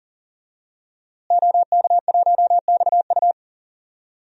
Morse code, heard as a keyed tone, about 40 wpm, 700 Hz. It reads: OK1XU
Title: DXCC-100 Call Signs - 5 Characters - Repeated 2x - 40wpm